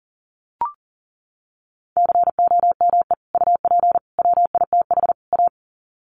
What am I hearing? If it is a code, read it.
Morse code, heard as a keyed tone, about 40 wpm, 700 Hz. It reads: COME UP WITH A
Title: Sets of 4 Words 40wpm